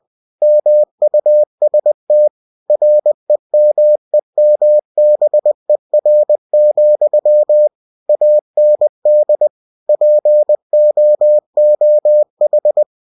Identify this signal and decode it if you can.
Morse code, heard as a keyed tone, about 20 wpm, 600 Hz. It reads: MUST REMEMBER, AND POOH
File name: cw-books / winnie-the-pooh / winnie-20wpm-page-0034